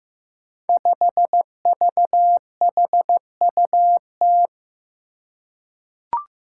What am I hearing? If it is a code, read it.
Morse code, heard as a keyed tone, about 15 wpm, 700 Hz. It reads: 5VHUT
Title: Five Challenging Letter-Number Combinations 15wpm